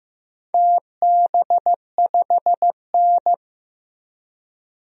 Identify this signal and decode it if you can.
Morse code, heard as a keyed tone, about 15 wpm, 700 Hz. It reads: TB5N
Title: DXCC-100 Call Signs - 4 Characters - Repeated 3x - 15wpm